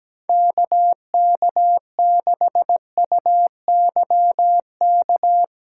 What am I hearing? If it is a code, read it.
Morse code, heard as a keyed tone, about 17 wpm, 700 Hz. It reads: KK6UYK
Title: USA Call Signs - 6x Character Spacing - 17wpm